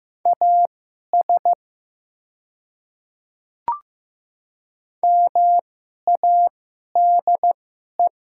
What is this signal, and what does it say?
Morse code, heard as a keyed tone, about 15 wpm, spaced as for 11 wpm, 700 Hz. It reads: AS MADE
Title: 2x Character Spacing - Top 200 Words - 15wpm